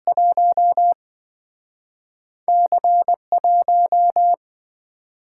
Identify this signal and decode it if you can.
Morse code, heard as a keyed tone, about 27 wpm, spaced as for 5 wpm, 700 Hz. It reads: 1C 1